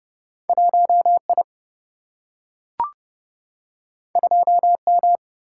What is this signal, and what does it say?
Morse code, heard as a keyed tone, about 30 wpm, 700 Hz. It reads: 1 I 2 M
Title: Single-Letter Single-Number 30wpm